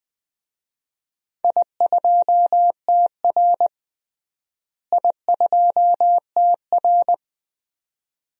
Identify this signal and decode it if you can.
Morse code, heard as a keyed tone, about 20 wpm, 700 Hz. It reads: I2TR I2TR
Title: DXCC-100 Call Signs - 4 Characters - Repeated 3x - 20wpm